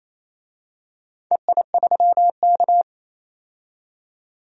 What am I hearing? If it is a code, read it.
Morse code, heard as a keyed tone, about 28 wpm, 700 Hz. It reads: EI3K
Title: DXCC-100 Call Signs - 4 Characters - 28wpm